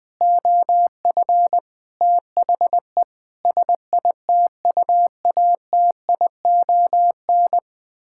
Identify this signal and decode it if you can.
Morse code, heard as a keyed tone, about 20 wpm, 700 Hz. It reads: OF THE SITUATION
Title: Sentences with 9-Word Phrases - Spoken Beginning - 20wpm